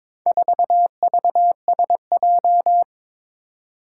Morse code, heard as a keyed tone, about 22 wpm, 700 Hz. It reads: 4VSJ
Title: Four Challenging Letter-Number Combinations 22wpm